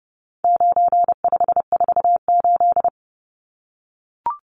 Morse code, heard as a keyed tone, about 30 wpm, 700 Hz. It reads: 9548
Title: Numbers - 4-Digit 30wpm